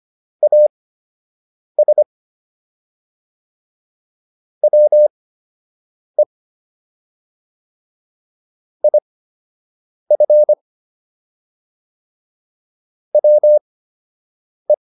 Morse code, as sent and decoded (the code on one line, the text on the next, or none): .- ... / .-- . / .. ..-. / .-- .
AS WE IF WE